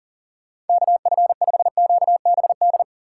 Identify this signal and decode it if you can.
Morse code, heard as a keyed tone, about 40 wpm, 700 Hz. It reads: KF5QBD
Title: USA Call Signs - 6x Character Spacing - 40wpm